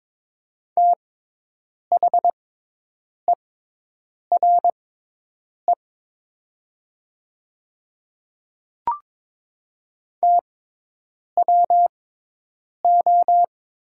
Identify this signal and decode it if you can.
Morse code, heard as a keyed tone, about 22 wpm, spaced as for 8 wpm, 700 Hz. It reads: THERE TWO
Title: 6x Character Spacing - Top 300 Words - 22wpm